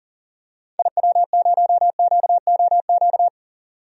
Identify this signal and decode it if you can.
Morse code, heard as a keyed tone, about 40 wpm, 700 Hz. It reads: IW0QOQ
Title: DXCC-100 Call Signs - 6 Characters - 40wpm